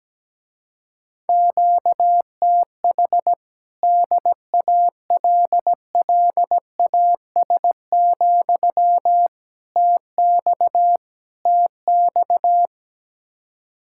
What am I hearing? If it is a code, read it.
Morse code, heard as a keyed tone, about 17 wpm, 700 Hz. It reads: QTH DALLAS, TX TX
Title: Speed-Racing Common QSO Phrases - 17wpm